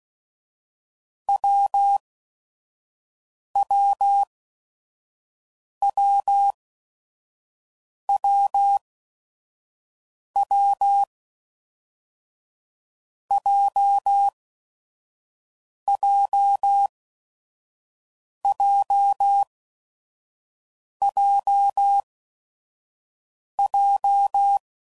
Morse code, heard as a keyed tone, about 16 wpm, spaced as for 5 wpm, 800 Hz. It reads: WWWWW JJJJJ
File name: Combinations / Difficult Combos / AWJ1